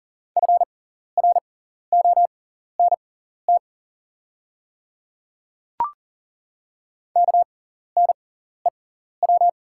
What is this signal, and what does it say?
Morse code, heard as a keyed tone, about 40 wpm, spaced as for 14 wpm, 690 Hz. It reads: FRONT KNEW
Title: 6x Character Spacing - Top 500 Words - 40wpm